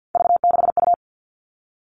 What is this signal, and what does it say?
Morse code, heard as a keyed tone, about 50 wpm, 700 Hz. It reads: V6U
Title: Three Challenging Letter-Number Combinations 50wpm